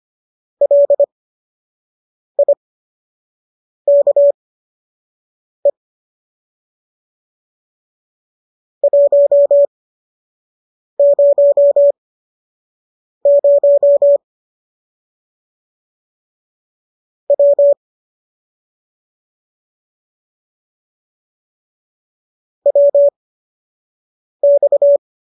Morse code, heard as a keyed tone, about 25 wpm, spaced as for 6 wpm, 570 Hz. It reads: LIKE 100 W WX